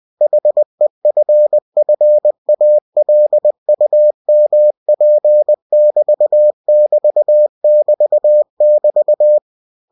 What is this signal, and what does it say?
Morse code, heard as a keyed tone, about 20 wpm, 600 Hz. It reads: HEFFALUMP <BT> <BT> <BT> <BT>
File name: cw-books / winnie-the-pooh / winnie-20wpm-page-0011